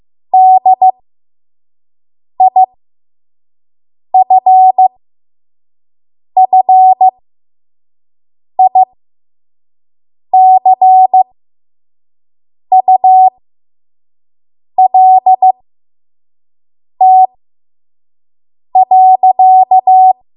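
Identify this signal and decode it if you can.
Morse code, heard as a keyed tone, about 15 wpm, spaced as for 5 wpm, 750 Hz. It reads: DIFFICULT.
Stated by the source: 5 WPM Morse code practice.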